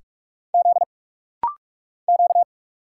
Morse code, heard as a keyed tone, about 46 wpm, 700 Hz. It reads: Z Q
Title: Single Letters - Rapid-Fire 45wpm